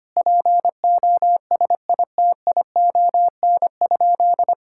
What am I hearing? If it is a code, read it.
Morse code, heard as a keyed tone, about 25 wpm, 700 Hz. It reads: POSITION?
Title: Sentences from Top 1000 Words - Review Part 5 - 25wpm